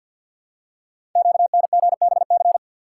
Morse code, heard as a keyed tone, about 51 wpm, 700 Hz. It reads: QNGBC
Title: Code Groups - 5 Letters - 50wpm